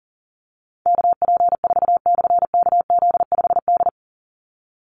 Morse code, heard as a keyed tone, about 40 wpm, 700 Hz. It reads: KP4/KZ5D
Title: Call Signs - 40wpm